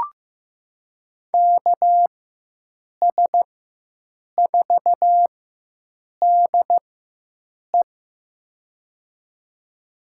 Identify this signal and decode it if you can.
Morse code, heard as a keyed tone, about 15 wpm, spaced as for 7 wpm, 700 Hz. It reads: KS4DE
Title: USA Call Signs - 5 Characters - 4x Character Spacing - 15wpm